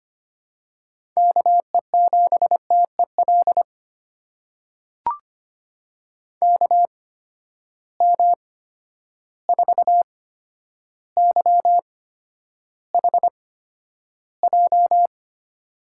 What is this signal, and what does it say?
Morse code, heard as a keyed tone, about 25 wpm, spaced as for 11 wpm, 700 Hz. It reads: KE7TEL KM4YHJ
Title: USA Call Signs - 6 Characters - 8x Character Spacing - 25wpm